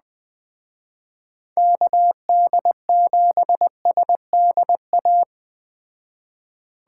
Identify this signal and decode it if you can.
Morse code, heard as a keyed tone, about 20 wpm, 700 Hz. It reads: KD7SDA
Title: Speed Racing - USA 2x3 Call Signs 17wpm